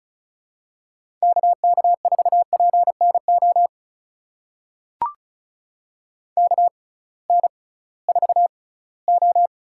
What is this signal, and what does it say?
Morse code, heard as a keyed tone, about 34 wpm, spaced as for 20 wpm, 700 Hz. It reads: KK4PNO KN4O
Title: USA Call Signs - 6 Characters - 6x Character Spacing - 35wpm